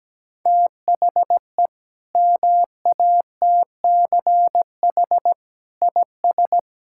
Morse code, heard as a keyed tone, about 17 wpm, 700 Hz. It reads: THE MATCH IS
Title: Sentences from Top 1600 Words 17wpm